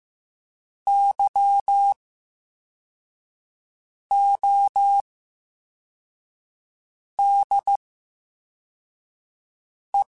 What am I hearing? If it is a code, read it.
Morse code, heard as a keyed tone, about 15 wpm, spaced as for 4 wpm, 800 Hz. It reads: YODE
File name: Special INGs-mp3s / Special INGs-mp3s / VerySlowToFaster Ing @13